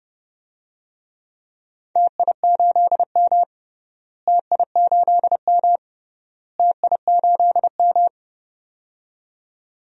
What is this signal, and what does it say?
Morse code, heard as a keyed tone, about 30 wpm, 700 Hz. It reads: TI8M TI8M TI8M
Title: DXCC-100 Call Signs - 4 Characters - Repeated 3x - 30wpm